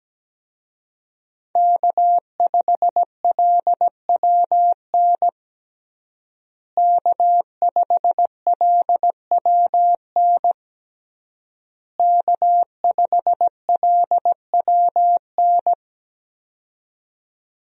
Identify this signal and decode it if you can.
Morse code, heard as a keyed tone, about 17 wpm, 700 Hz. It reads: K5LWN K5LWN K5LWN
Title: USA Call Signs - 5 Characters - Repeated 3x - 17wpm